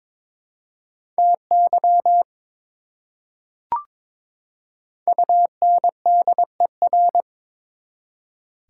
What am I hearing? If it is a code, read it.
Morse code, heard as a keyed tone, about 22 wpm, 700 Hz. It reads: TY UNDER
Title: Prefixes and Suffixes 22wpm